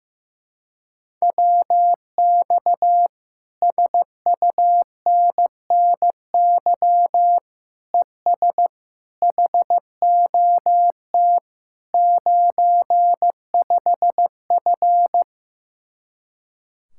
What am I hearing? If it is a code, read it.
Morse code, heard as a keyed tone, about 15 wpm, 700 Hz. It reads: WX SUNNY ES HOT 95F
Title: Speed-Racing Common QSO Phrases Encore - 15wpm